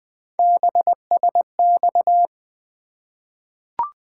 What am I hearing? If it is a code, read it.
Morse code, heard as a keyed tone, about 20 wpm, 700 Hz. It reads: BSX
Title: Three Letter-Number Combinations - 20wpm